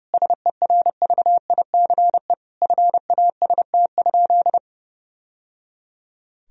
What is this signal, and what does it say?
Morse code, heard as a keyed tone, about 30 wpm, 700 Hz. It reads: SERVICE FAST?
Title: Sentences from Top 1000 Words - Review Part 4 - 30wpm